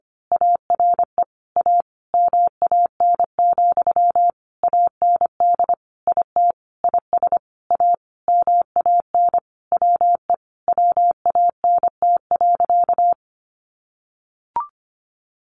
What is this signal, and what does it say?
Morse code, heard as a keyed tone, about 25 wpm, 700 Hz. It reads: ARE A MAN, AND IT IS A MAN WE WANT.
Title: Sentences from Top 200 Words 25wpm